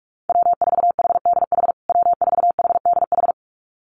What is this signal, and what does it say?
Morse code, heard as a keyed tone, about 44 wpm, 700 Hz. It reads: W4HDH W4HDH
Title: USA Call Signs Repeated 2x 45wpm